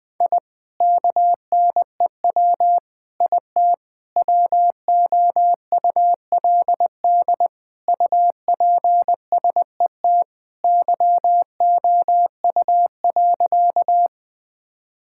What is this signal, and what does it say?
Morse code, heard as a keyed tone, about 20 wpm, 700 Hz. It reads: I KNEW IT WOULD UPSET YOU.
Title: Sentences from Top 1000 Words - Review Part 5 - 20wpm